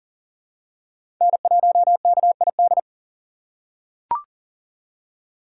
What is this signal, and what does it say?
Morse code, heard as a keyed tone, about 40 wpm, 700 Hz. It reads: N1KID
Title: USA Call Signs - 5 Characters - 3x Character Spacing - 40wpm